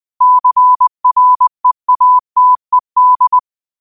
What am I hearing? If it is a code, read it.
Morse code, heard as a keyed tone, about 20 wpm, 1000 Hz. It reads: CREATED